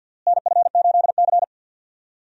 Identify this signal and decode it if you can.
Morse code, heard as a keyed tone, about 47 wpm, 700 Hz. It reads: NF8C
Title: USA Call Signs - 4 Characters - Repeated 3x - 50wpm